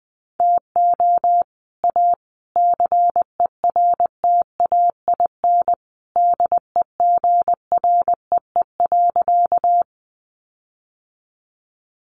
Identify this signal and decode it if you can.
Morse code, heard as a keyed tone, about 20 wpm, 700 Hz. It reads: TO A CERTAIN DEGREE.
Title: Sentences from Top 1500 Words 20wpm